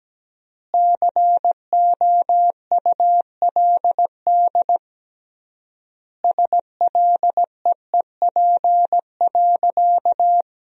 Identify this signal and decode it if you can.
Morse code, heard as a keyed tone, about 17 wpm, 700 Hz. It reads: COULD SLEEP.
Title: 3x Word-Spacing - Sentences from Top 500 Words - 17wpm